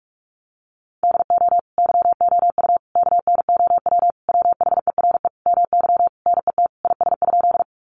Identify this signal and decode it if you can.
Morse code, heard as a keyed tone, about 44 wpm, 700 Hz. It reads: DO YOU KNOW WHERE MY NET IS?
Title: Sentences from Top 1400 Words 45wpm